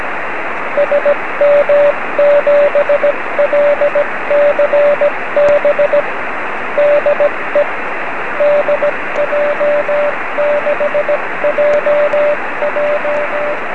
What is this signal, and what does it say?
Morse code, heard as a keyed tone, about 16 wpm, 580 Hz, with radio noise. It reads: SM7LCB DE DJ6JJ
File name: 1918z dj6ff 9cm